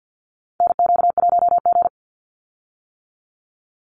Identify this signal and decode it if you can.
Morse code, heard as a keyed tone, about 50 wpm, 700 Hz. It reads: NQ1G
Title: USA Call Signs 50wpm